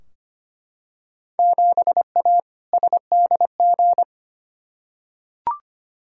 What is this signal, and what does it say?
Morse code, heard as a keyed tone, about 25 wpm, 700 Hz. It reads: 7A SDG
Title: Field Day Exchanges Encore - 25wpm